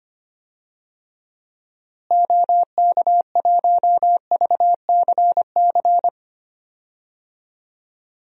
Morse code, heard as a keyed tone, about 25 wpm, 700 Hz. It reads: OK1VCC
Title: DXCC-100 Call Signs - 6 Characters - Repeated 2x - 25wpm